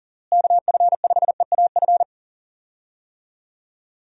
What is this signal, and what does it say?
Morse code, heard as a keyed tone, about 37 wpm, 700 Hz. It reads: KF5EAF
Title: USA Call Signs - 6 Characters - Repeated 2x - 40wpm